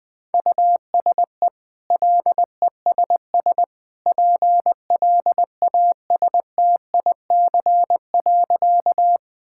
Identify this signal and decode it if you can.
Morse code, heard as a keyed tone, about 21 wpm, 700 Hz. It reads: USE LESS PLASTIC.
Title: Easy Sentences with Hints - 20wpm